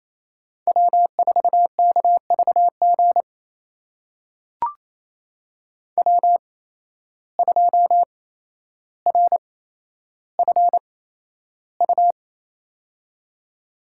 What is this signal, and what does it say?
Morse code, heard as a keyed tone, about 28 wpm, spaced as for 12 wpm, 700 Hz. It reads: W4KVG W2RFU
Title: USA Call Signs - 8x Character Spacing - 28wpm